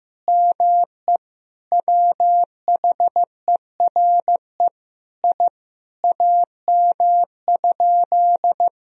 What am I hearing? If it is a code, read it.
Morse code, heard as a keyed tone, about 15 wpm, 700 Hz. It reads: ME WHERE I AM?
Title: Sentences from Top 1000 Words - Review Part 2 - 15wpm